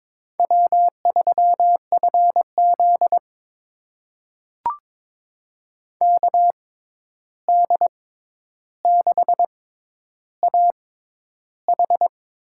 Morse code, heard as a keyed tone, about 22 wpm, spaced as for 10 wpm, 700 Hz. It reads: W3FZ KD6AH
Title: USA Call Signs - 6x Character Spacing - 22wpm